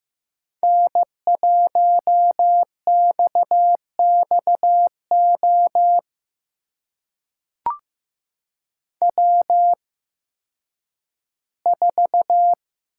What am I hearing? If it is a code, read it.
Morse code, heard as a keyed tone, about 15 wpm, spaced as for 10 wpm, 700 Hz. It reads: N1XXO W4
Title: USA Call Signs - 5 Characters - 8x Character Spacing - 15wpm